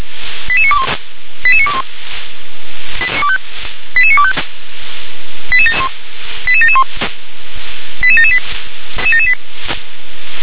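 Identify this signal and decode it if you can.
Five-tone selective call (ZVEI), heard as quick runs of five tones, tones of about 70 ms. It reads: 88921 88921 88925 88771 88778
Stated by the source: ZVEI five-tone call 88952 received by SDR.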